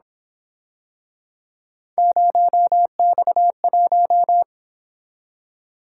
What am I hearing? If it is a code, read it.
Morse code, heard as a keyed tone, about 27 wpm, 700 Hz. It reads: 0X1
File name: Speed-Racing - Three Letter-Number Combinations - 22wpm